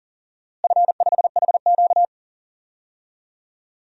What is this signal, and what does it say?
Morse code, heard as a keyed tone, about 38 wpm, 700 Hz. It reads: F5HQ
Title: DXCC-100 Call Signs - 4 Characters - Repeated 2x - 40wpm